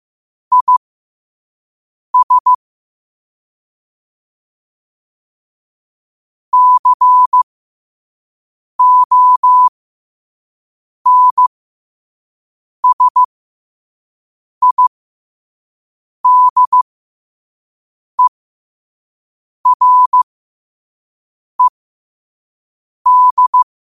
Morse code, heard as a keyed tone, about 15 wpm, spaced as for 5 wpm, 1000 Hz. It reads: IS CONSIDERED
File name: qotdfiles / 5wpm